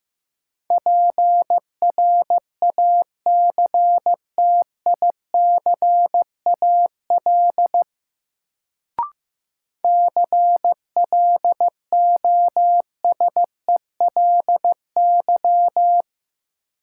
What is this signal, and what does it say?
Morse code, heard as a keyed tone, about 15 wpm, 700 Hz. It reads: PRACTICAL CLOSELY
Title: Sentences from Top 1400 Words - New Word Drill - 15wpm